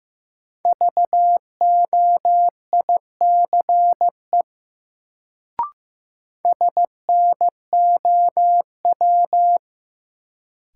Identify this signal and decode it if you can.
Morse code, heard as a keyed tone, about 15 wpm, 700 Hz. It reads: VOICE SNOW
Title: Sentences from Top 500 Words - New Word Drill - 15wpm